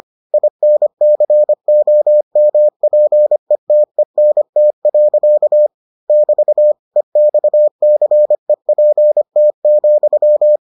Morse code, heard as a keyed tone, about 25 wpm, 600 Hz. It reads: INCOMPETENT. <BT> EXCEPT,